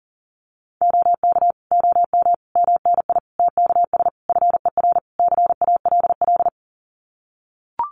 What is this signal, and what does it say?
Morse code, heard as a keyed tone, about 37 wpm, 700 Hz. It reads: OK OM MNI TKS FER CALL
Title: Common QSO Phrases 40wpm